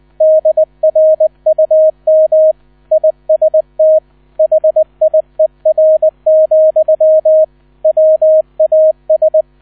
Morse code, heard as a keyed tone, about 20 wpm, 630 Hz, with radio noise. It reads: DRUM IST HIER, WAS
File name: 000 Vorwort 20wpm